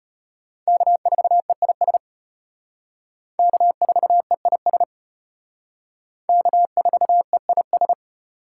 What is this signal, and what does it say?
Morse code, heard as a keyed tone, about 32 wpm, 700 Hz. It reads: K4EIS K4EIS K4EIS